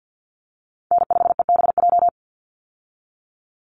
Morse code, heard as a keyed tone, about 50 wpm, 700 Hz. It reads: N5EBJ
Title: USA Call Signs 50wpm